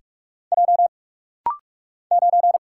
Morse code, heard as a keyed tone, about 44 wpm, 700 Hz. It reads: J 9
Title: Single Letter-Number Alt - Rapid-Fire 45wpm